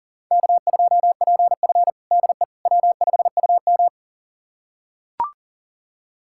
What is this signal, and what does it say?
Morse code, heard as a keyed tone, about 40 wpm, 700 Hz. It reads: K2PF DE W5UM
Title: USA Callsign DE Callsign - 4 Characters - 40wpm